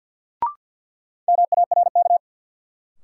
Morse code, heard as a keyed tone, about 53 wpm, 700 Hz. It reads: MARK
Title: Sentences from Top 300 Words - New Word Drill - 50wpm